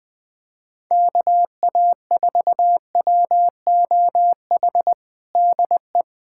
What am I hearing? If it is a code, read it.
Morse code, heard as a keyed tone, about 21 wpm, 700 Hz. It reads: KA4WOH DE
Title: USA Callsign DE Callsign 20wpm